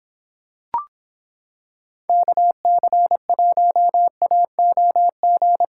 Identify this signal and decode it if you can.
Morse code, heard as a keyed tone, about 26 wpm, 700 Hz. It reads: KC1AOG